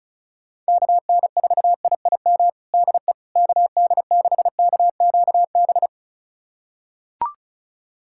Morse code, heard as a keyed tone, about 36 wpm, 700 Hz. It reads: KN4IIM DE KD6KQB
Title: USA Callsign DE Callsign 35wpm